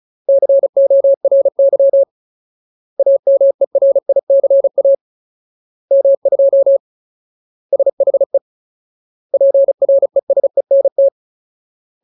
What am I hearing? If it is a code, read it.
Morse code, heard as a keyed tone, about 34 wpm, 550 Hz. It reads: CORY AMERICA M2 SHE PRESENT
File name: Ultimate Word List / 35WPM Ultimate